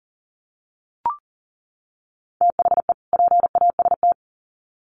Morse code, heard as a keyed tone, about 40 wpm, 700 Hz. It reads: THE PAST
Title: Sets of 2 Words 40wpm